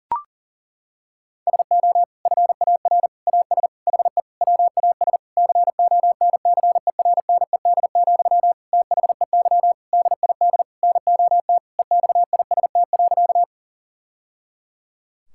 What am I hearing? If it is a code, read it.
Morse code, heard as a keyed tone, about 40 wpm, 700 Hz. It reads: SO FAR AS HE WAS CONCERNED, THEY DID NOT EXIST.
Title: Sentences from Top 1000 Words - Review Part 6 - 40wpm